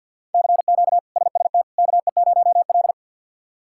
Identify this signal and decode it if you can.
Morse code, heard as a keyed tone, about 46 wpm, 690 Hz. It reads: CQ SST KE0L
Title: K1USN SST QSO Practice - 50wpm